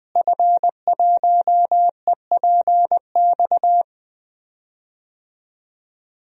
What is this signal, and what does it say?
Morse code, heard as a keyed tone, about 21 wpm, 700 Hz. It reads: F1EPX
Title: DXCC-100 Call Signs - 5 Characters - Repeated 2x - 20wpm